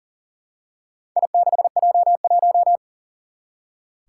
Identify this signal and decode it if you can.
Morse code, heard as a keyed tone, about 40 wpm, 700 Hz. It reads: I6J1
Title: Four Challenging Letter-Number Combinations 40wpm